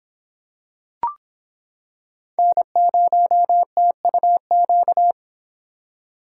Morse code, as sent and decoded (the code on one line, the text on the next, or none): -. ----- - ..- --.-
N0TUQ